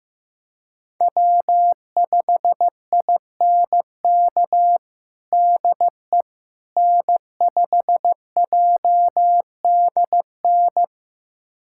Morse code, heard as a keyed tone, about 15 wpm, 700 Hz. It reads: W5INK DE N5JDN
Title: USA Callsign DE Callsign - 5 Characters - 15wpm